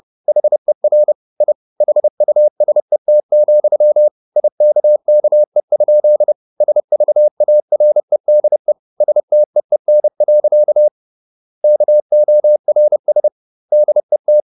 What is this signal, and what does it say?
Morse code, heard as a keyed tone, about 30 wpm, 600 Hz. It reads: HER I HUSET, IKKE? SVAREDE STEEN. KORS DET
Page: Ferie 30 wpm